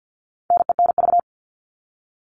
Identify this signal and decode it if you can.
Morse code, heard as a keyed tone, about 50 wpm, 700 Hz. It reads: NENV
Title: All Cut Numbers - 4-Digit 50wpm